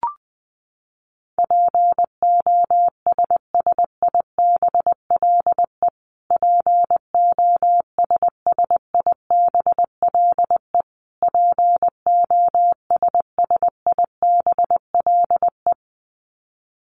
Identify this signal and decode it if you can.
Morse code, heard as a keyed tone, about 20 wpm, 700 Hz. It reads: POSSIBLE POSSIBLE POSSIBLE
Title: Top 500 Words Repeated 20wpm